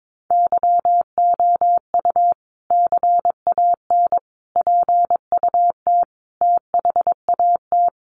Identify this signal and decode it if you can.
Morse code, heard as a keyed tone, about 22 wpm, 700 Hz. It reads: YOU CAN PUT THAT
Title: Sentences from Top 1500 Words 22wpm